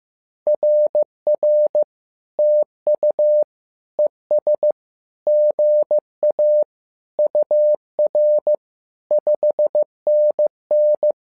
Morse code, heard as a keyed tone, about 15 wpm, 600 Hz. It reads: RR TU ES GA UR 5NN
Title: SOTA QSO Practice 15wpm